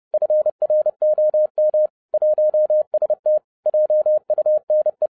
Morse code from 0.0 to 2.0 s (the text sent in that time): FROM